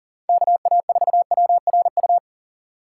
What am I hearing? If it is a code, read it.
Morse code, heard as a keyed tone, about 40 wpm, 700 Hz. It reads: KA4WRU
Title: USA Call Signs - 6 Characters - 4x Character Spacing - 40wpm